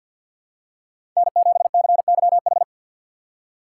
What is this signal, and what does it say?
Morse code, heard as a keyed tone, about 50 wpm, 700 Hz. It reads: N7CYH
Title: USA Call Signs - 5 Characters - 3x Character Spacing - 50wpm